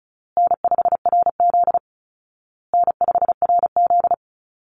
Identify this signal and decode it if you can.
Morse code, heard as a keyed tone, about 34 wpm, 700 Hz. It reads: N5RZ N5RZ
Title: Easy Call Signs Repeated 3x - 35wpm